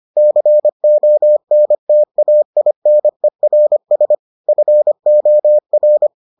Morse code, heard as a keyed tone, about 25 wpm, 600 Hz. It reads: CONTAINERS FOR